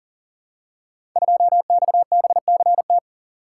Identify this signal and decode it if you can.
Morse code, heard as a keyed tone, about 43 wpm, 690 Hz. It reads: 2XBCT
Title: Code Groups - 5 Letter:Numbers - 40wpm